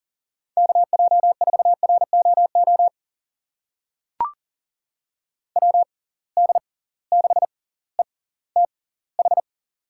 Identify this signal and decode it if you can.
Morse code, heard as a keyed tone, about 40 wpm, spaced as for 20 wpm, 700 Hz. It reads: KJ4ROO WD6ETH
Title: USA Call Signs - 6 Characters - 6x Character Spacing - 40wpm